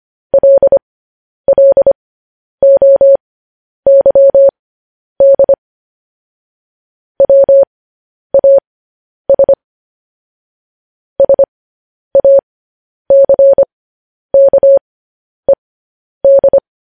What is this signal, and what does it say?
Morse code, heard as a keyed tone, about 25 wpm, spaced as for 10 wpm, 550 Hz. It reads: LLOYD WAS SACKED
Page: News Headlines in Morse Code at 10 WPM